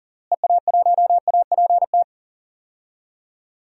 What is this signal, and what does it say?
Morse code, heard as a keyed tone, about 40 wpm, 700 Hz. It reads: EA1APT
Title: DXCC-100 Call Signs - 6 Characters - Repeated 3x - 40wpm